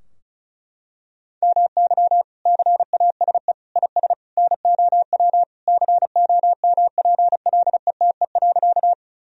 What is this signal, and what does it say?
Morse code, heard as a keyed tone, about 36 wpm, 700 Hz. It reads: MY CASE IS NOW COMPLETE.
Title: Sentences from Top 1000 Words - Review Part 2 - 35wpm